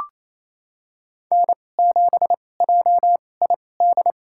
Morse code, heard as a keyed tone, about 27 wpm, spaced as for 20 wpm, 700 Hz. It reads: N7JID
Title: USA Call Signs - 5 Characters - 2x Character Spacing - 28wpm